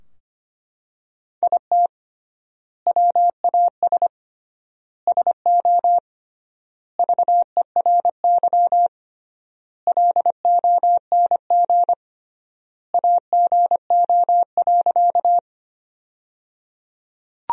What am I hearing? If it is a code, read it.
Morse code, heard as a keyed tone, about 25 wpm, 700 Hz. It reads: IT WAS SO VERY LONG AGO.
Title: 3x Word-Spacing - Sentences from Top 500 Words - 25wpm